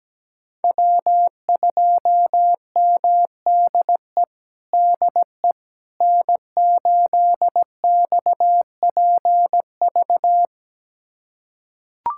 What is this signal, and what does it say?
Morse code, heard as a keyed tone, about 17 wpm, 700 Hz. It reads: W2MDE DE N8XPV
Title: USA Callsign DE Callsign - 5 Characters - 17wpm